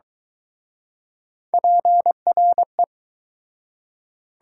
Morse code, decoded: PRE

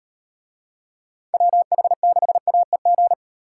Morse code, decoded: WH6AEG

